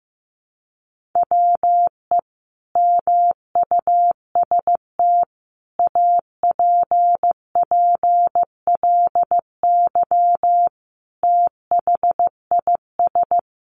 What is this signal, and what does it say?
Morse code, heard as a keyed tone, about 15 wpm, 700 Hz. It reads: WE MUST APPLY THIS
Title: Sentences from Top 1400 Words 15wpm